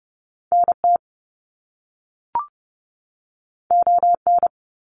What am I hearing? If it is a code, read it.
Morse code, heard as a keyed tone, about 30 wpm, 700 Hz. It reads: NT ON
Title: Canadian Provinces and Territories 30wpm